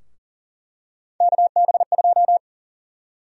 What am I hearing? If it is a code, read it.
Morse code, heard as a keyed tone, about 40 wpm, 700 Hz. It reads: KB2
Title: USA Call Sign Prefixes - Repeated 2x - 40wpm